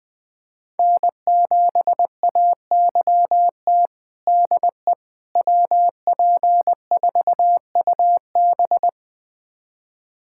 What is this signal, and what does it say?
Morse code, heard as a keyed tone, about 20 wpm, 700 Hz. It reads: N7AYT DE WP4UB
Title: USA Callsign DE Callsign - 5 Characters - 20wpm